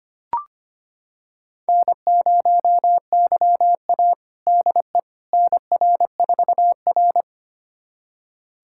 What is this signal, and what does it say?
Morse code, heard as a keyed tone, about 25 wpm, 700 Hz. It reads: N0YA DE NR4R
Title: USA Callsign DE Callsign - 4 Characters - 25wpm